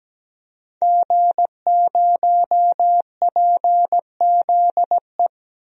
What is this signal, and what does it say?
Morse code, heard as a keyed tone, about 17 wpm, 700 Hz. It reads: G0PZE
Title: DXCC-100 Call Signs - 5 Characters - 17wpm